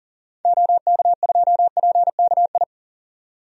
Morse code, heard as a keyed tone, about 40 wpm, 700 Hz. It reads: OK2PKI
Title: DXCC-100 Call Signs - 6 Characters - 40wpm